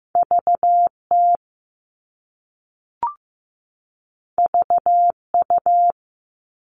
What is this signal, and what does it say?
Morse code, heard as a keyed tone, about 15 wpm, 700 Hz. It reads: VT VU
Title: All Cut Numbers - 2-Digit 15wpm